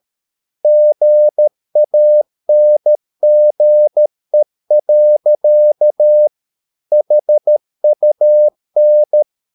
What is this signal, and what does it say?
Morse code, heard as a keyed tone, about 13 wpm, 600 Hz. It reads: GANGE. HUN
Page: Ferie 13 wpm